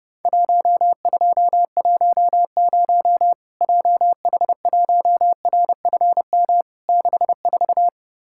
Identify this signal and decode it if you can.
Morse code, heard as a keyed tone, about 30 wpm, 700 Hz. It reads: 1210 JH1RFM 64
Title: Worked All Europe DX Contest - QTC Exchange - 30wpm